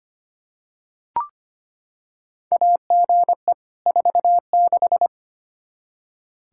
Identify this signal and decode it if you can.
Morse code, heard as a keyed tone, about 25 wpm, 700 Hz. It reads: AGE 46
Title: K1USN SST QSO Practice 25wpm